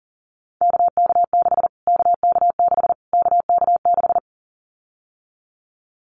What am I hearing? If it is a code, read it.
Morse code, heard as a keyed tone, about 40 wpm, 700 Hz. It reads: KK6 KK6 KK6
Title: Callsign Prefixes for Top 25 DXCC - Repeated 3x - 40wpm